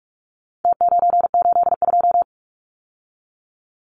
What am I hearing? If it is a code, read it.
Morse code, heard as a keyed tone, about 44 wpm, 700 Hz. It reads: T982
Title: Numbers - Contest Style - 4-Digit 45wpm